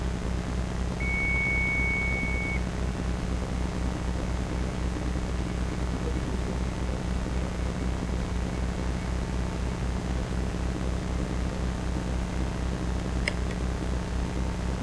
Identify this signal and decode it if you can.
Dial-up modem handshake.